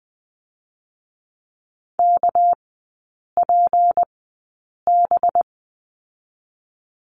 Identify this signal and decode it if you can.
Morse code, heard as a keyed tone, about 20 wpm, spaced as for 8 wpm, 690 Hz. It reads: KPB
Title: 198 Lesson 32e - Letter k - Three Characters 20wpm